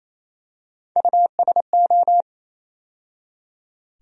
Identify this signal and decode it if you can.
Morse code, heard as a keyed tone, about 28 wpm, 700 Hz. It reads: USO
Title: USA Call Sign Postfixes - 3 Letters 28wpm